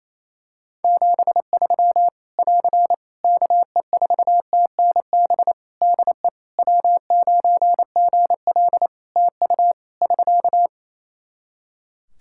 Morse code, heard as a keyed tone, about 28 wpm, 700 Hz. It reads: 73 <AR> KE4TNB DE W9GL TU <SK>
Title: K1USN SST QSO Practice 28wpm